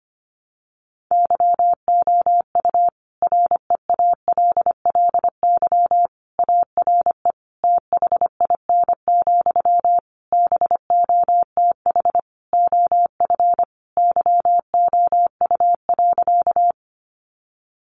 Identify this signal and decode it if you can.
Morse code, heard as a keyed tone, about 25 wpm, 700 Hz. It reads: YOU REALLY ARE THIN, BOTH OF YOU.
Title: Sentences from Top 1200 Words 25wpm